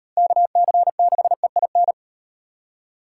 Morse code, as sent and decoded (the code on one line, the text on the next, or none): -.- -.-. -.... .. -.
KC6EIN